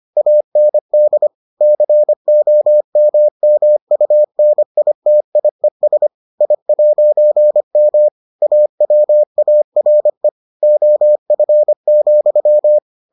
Morse code, heard as a keyed tone, about 25 wpm, 600 Hz. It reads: AND COMMUNITIES I'M AWARE OF,